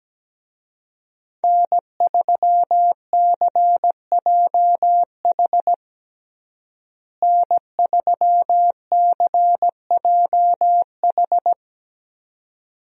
Morse code, heard as a keyed tone, about 17 wpm, 700 Hz. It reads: N3CJH N3CJH
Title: USA Call Signs - 5 Characters - Repeated 2x - 17wpm